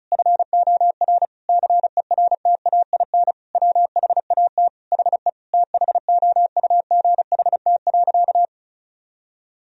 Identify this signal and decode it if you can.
Morse code, heard as a keyed tone, about 34 wpm, 700 Hz. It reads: FOR CERTAIN WHAT HE THOUGHT.
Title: Sentences from Top 1000 Words - Review Part 2 - 35wpm